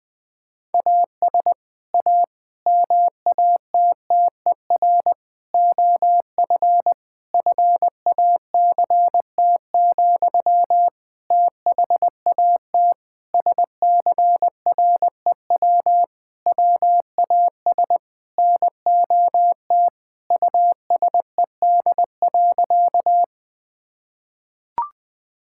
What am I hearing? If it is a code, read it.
Morse code, heard as a keyed tone, about 20 wpm, 700 Hz. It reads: AS A MATTER OF FACT, THAT SCREW WAS NOT USED.
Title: Sentences from Top 1600 Words 20wpm